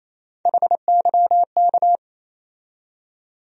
Morse code, heard as a keyed tone, about 28 wpm, 700 Hz. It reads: HYK
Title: USA Call Sign Postfixes - 3 Letters 28wpm